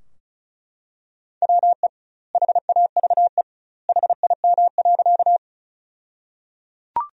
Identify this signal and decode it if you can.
Morse code, heard as a keyed tone, about 35 wpm, 700 Hz. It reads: WE HAVE HIM.
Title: 2x Word-Spacing - Sentences from Top 100 Words - 35wpm